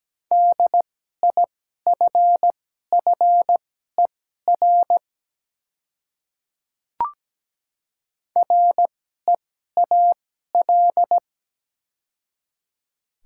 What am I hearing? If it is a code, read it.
Morse code, heard as a keyed tone, about 17 wpm, spaced as for 12 wpm, 700 Hz. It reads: DIFFER REAL
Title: 2x Character Spacing - Top 300 Words - 17wpm